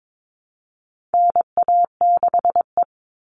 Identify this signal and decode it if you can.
Morse code, heard as a keyed tone, about 22 wpm, 700 Hz. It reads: NA6E
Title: All Cut Numbers - 4-Digit 22wpm